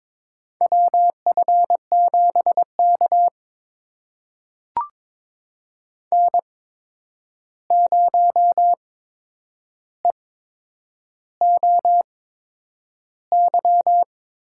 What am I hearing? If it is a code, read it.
Morse code, heard as a keyed tone, about 22 wpm, spaced as for 9 wpm, 700 Hz. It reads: WF7K N0EOY
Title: USA Call Signs - 8x Character Spacing - 22wpm